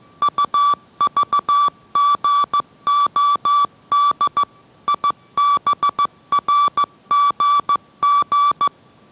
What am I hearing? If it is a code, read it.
Morse code, heard as a keyed tone, about 15 wpm, 1160 Hz. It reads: UVGOD IBRGG